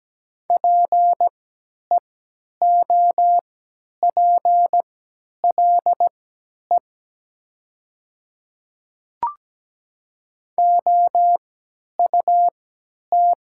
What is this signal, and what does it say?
Morse code, heard as a keyed tone, about 17 wpm, spaced as for 10 wpm, 700 Hz. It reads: PEOPLE OUT
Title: 3x Character Spacing - Top 200 Words - 17wpm